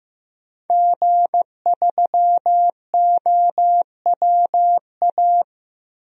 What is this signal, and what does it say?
Morse code, heard as a keyed tone, about 15 wpm, 700 Hz. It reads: G3OWA
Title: DXCC-100 Call Signs - 5 Characters - Repeated 3x - 15wpm